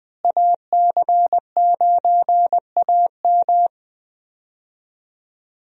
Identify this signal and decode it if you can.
Morse code, heard as a keyed tone, about 21 wpm, 700 Hz. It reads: AC9AM
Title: USA Call Signs - 5 Characters - Repeated 2x - 20wpm